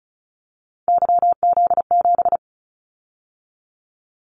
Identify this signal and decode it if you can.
Morse code, heard as a keyed tone, about 36 wpm, 700 Hz. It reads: YZ7
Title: Callsign Prefixes for Top 25 DXCC - 35wpm